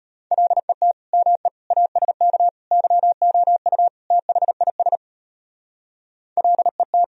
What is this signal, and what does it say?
Morse code, heard as a keyed tone, about 32 wpm, 700 Hz. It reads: LET ME ASK YOU THIS LET